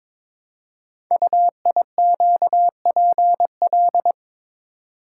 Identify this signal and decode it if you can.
Morse code, heard as a keyed tone, about 22 wpm, 700 Hz. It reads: UIQPL
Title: Code Groups - 5 Letters - 22wpm